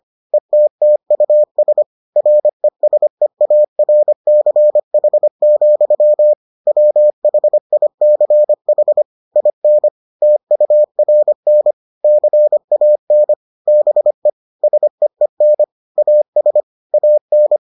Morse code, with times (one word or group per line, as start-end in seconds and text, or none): 0.3-1.8 ETTUS
2.2-6.3 RESEARCH,
6.7-9.0 WHICH
9.4-9.9 IN
10.2-11.7 TURN
12.0-13.3 CAN
13.7-14.3 BE
14.6-15.6 SEEN
16.0-16.6 AS
16.9-17.6 AN